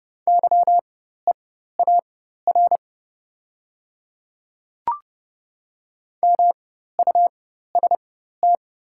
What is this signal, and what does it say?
Morse code, heard as a keyed tone, about 30 wpm, spaced as for 14 wpm, 700 Hz. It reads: YEAR MUST
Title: 4x Character Spacing - Top 200 Words - 30wpm